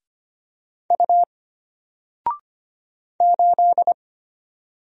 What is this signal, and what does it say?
Morse code, heard as a keyed tone, about 25 wpm, 700 Hz. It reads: U 8
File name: Single-Letter-Number-25wpm